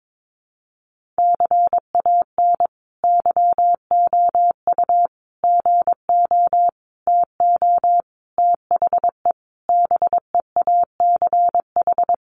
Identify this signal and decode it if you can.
Morse code, heard as a keyed tone, about 22 wpm, 700 Hz. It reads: CAN YOU GO TO THE BEACH
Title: Sentences from Top 1500 Words 22wpm